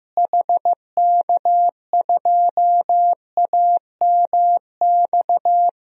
Morse code, heard as a keyed tone, about 16 wpm, 700 Hz. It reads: HK2AMX
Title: DXCC-100 Call Signs - 6 Characters - Repeated 2x - 15wpm